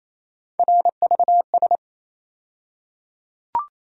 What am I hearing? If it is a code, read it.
Morse code, heard as a keyed tone, about 28 wpm, 700 Hz. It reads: RVS
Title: USA Call Sign Postfixes - 3 Letters 28wpm